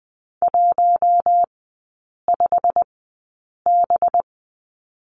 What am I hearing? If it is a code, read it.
Morse code, heard as a keyed tone, about 20 wpm, spaced as for 8 wpm, 700 Hz. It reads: 15B
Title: 191 Lesson 31e - Letter v - Three Characters 20wpm